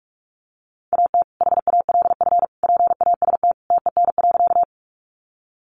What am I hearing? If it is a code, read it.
Morse code, heard as a keyed tone, about 44 wpm, 700 Hz. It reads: AT HALF PAST TEN.
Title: Sentences from Top 600 Words 45wpm